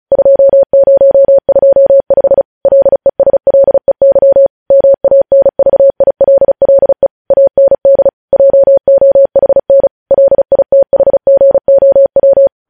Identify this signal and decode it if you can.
Morse code, heard as a keyed tone, about 36 wpm, 550 Hz. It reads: 2025 LESLEY MANVILLE AND JOHN LITHGOW
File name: podcast / News headlines-1x-35x35